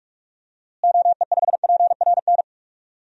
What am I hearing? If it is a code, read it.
Morse code, heard as a keyed tone, about 42 wpm, 700 Hz. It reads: OE5PRN
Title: DXCC-100 Call Signs - 6 Characters - Repeated 3x - 45wpm